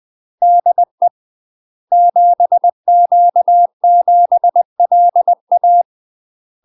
Morse code, heard as a keyed tone, about 20 wpm, 700 Hz. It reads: DE 7Q7LA
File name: UK MASTER QSOs TRAINING / 20WPM / UK MASTER QSOs 0035